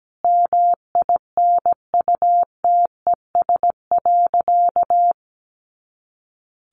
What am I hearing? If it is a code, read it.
Morse code, heard as a keyed tone, about 17 wpm, 700 Hz. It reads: MINUTES.
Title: A Sound Of Thunder 17wpm